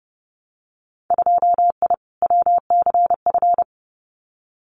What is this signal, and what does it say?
Morse code, heard as a keyed tone, about 32 wpm, 700 Hz. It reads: 2I WCF
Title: Winter Field Day Exchanges 30wpm